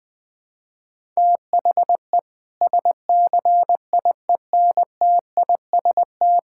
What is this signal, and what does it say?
Morse code, heard as a keyed tone, about 21 wpm, 700 Hz. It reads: THE SCIENTIST
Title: Sentences with 8-Word Phrases - Spoken Beginning - 20wpm